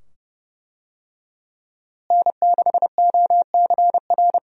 Morse code, heard as a keyed tone, about 29 wpm, 700 Hz. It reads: N6OCR
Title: USA Call Signs - 5 Characters - Repeated 2x - 30wpm